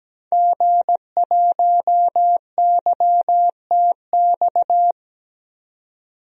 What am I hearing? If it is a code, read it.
Morse code, heard as a keyed tone, about 17 wpm, 700 Hz. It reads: G1YTX
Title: DXCC-100 Call Signs - 5 Characters - 17wpm